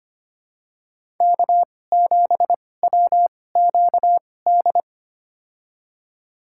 Morse code, heard as a keyed tone, about 25 wpm, spaced as for 18 wpm, 700 Hz. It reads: K7WQD
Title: USA Call Signs - 5 Characters - 2x Character Spacing - 25wpm